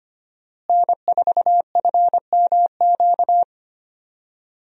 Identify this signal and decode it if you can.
Morse code, heard as a keyed tone, about 25 wpm, 700 Hz. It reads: N4FMQ
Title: USA Call Signs - 5 Characters - Repeated 3x - 25wpm